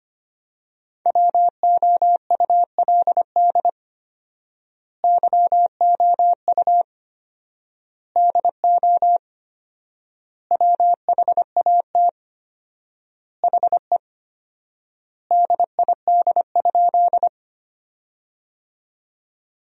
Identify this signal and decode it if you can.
Morse code, heard as a keyed tone, about 25 wpm, 700 Hz. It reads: WOULD YOU DO WHAT HE DID?
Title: 4x Word-Spacing - Sentences from Top 100 Words - 25wpm